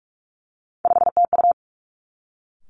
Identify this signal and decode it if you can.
Morse code, heard as a keyed tone, about 46 wpm, 700 Hz. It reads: 5TU
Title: Three Challenging Letter-Number Combinations 45wpm